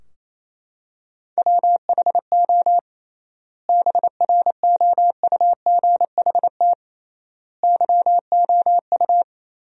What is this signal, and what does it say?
Morse code, heard as a keyed tone, about 28 wpm, 700 Hz. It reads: WHO BROUGHT YOU
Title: 3x Word-Spacing - Sentences from Top 500 Words - 28wpm